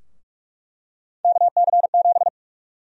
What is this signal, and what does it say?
Morse code, heard as a keyed tone, about 47 wpm, 700 Hz. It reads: KC7
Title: USA Call Sign Prefixes - Repeated 2x - 45wpm